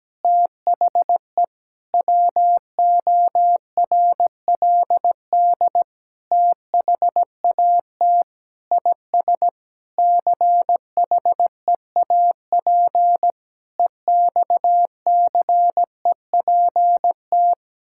Morse code, heard as a keyed tone, about 17 wpm, 700 Hz. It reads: THE WORLD THAT IS CHEAP EXCEPT
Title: Sentences from Top 1600 Words 17wpm